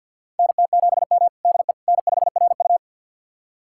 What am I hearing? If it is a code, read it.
Morse code, heard as a keyed tone, about 51 wpm, 700 Hz. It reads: NT7M DE N5RU
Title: USA Callsign DE Callsign - 4 Characters - 50wpm